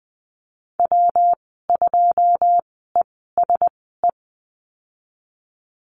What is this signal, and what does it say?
Morse code, heard as a keyed tone, about 20 wpm, spaced as for 15 wpm, 700 Hz. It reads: W2ESE
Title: 186 Lesson 30g - Letter b - Call Signs 20wpm